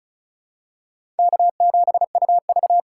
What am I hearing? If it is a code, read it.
Morse code, heard as a keyed tone, about 34 wpm, 700 Hz. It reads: K7UV
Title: USA Call Signs - 4 Characters - 4x Character Spacing - 35wpm